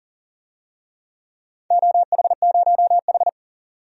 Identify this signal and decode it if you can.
Morse code, heard as a keyed tone, about 40 wpm, 700 Hz. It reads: OH0H
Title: DXCC-100 Call Signs - 4 Characters - 40wpm